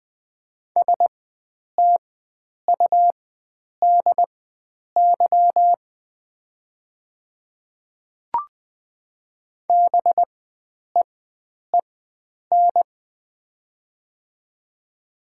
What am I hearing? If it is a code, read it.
Morse code, heard as a keyed tone, about 20 wpm, spaced as for 9 wpm, 700 Hz. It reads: STUDY BEEN
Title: 4x Character Spacing - Top 300 Words - 20wpm